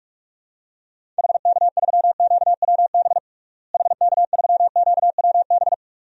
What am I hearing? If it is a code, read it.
Morse code, heard as a keyed tone, about 43 wpm, 690 Hz. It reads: HK3QWB HK3QWB
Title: DXCC-100 Call Signs - 6 Characters - Repeated 2x - 45wpm